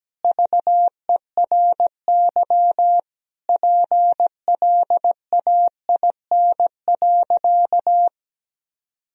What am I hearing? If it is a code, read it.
Morse code, heard as a keyed tone, about 17 wpm, 700 Hz. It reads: VERY PLAIN.
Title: Sentences from Top 1000 Words - Review Part 2 - 17wpm